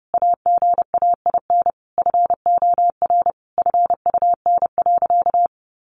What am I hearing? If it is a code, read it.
Morse code, heard as a keyed tone, about 30 wpm, 700 Hz. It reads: AGAIN FOR FUN.
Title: Easy Phrases 30wpm